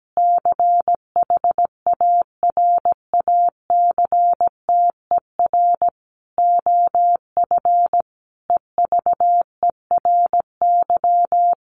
Morse code, heard as a keyed tone, about 17 wpm, 700 Hz. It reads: CHARACTER OF EVERY
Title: Sentences from Top 1400 Words 17wpm